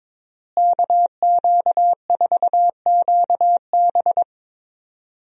Morse code, heard as a keyed tone, about 22 wpm, 700 Hz. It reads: KQ4QB
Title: USA Call Signs - 5 Characters - Repeated 2x - 22wpm